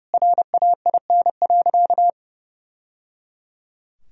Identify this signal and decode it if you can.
Morse code, heard as a keyed tone, about 30 wpm, 700 Hz. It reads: RAIN.
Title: Sentences from Top 1000 Words - Review Part 1 - 30wpm